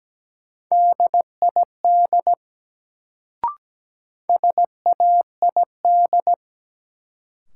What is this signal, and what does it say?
Morse code, heard as a keyed tone, about 17 wpm, 700 Hz. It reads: DID SAID
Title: Sentences from Top 100 Words - New Word Drill - 17wpm